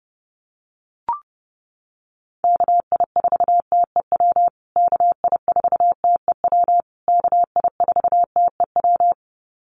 Morse code, heard as a keyed tone, about 30 wpm, 700 Hz. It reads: KI4TEW KI4TEW KI4TEW
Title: USA Call Signs Repeated 3x 30wpm